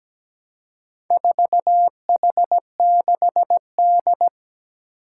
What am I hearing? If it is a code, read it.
Morse code, heard as a keyed tone, about 17 wpm, 700 Hz. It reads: 4H6D
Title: Four Challenging Letter-Number Combinations 17wpm